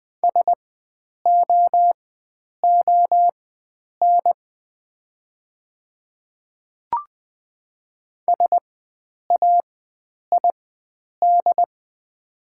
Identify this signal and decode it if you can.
Morse code, heard as a keyed tone, about 20 wpm, spaced as for 9 wpm, 700 Hz. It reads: SOON SAID
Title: 4x Character Spacing - Top 500 Words - 20wpm